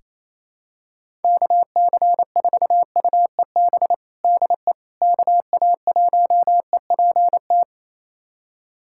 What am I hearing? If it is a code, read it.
Morse code, heard as a keyed tone, about 27 wpm, 700 Hz. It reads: KC4UEB DE KA1EPT
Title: USA Callsign DE Callsign - 6 Characters - 28wpm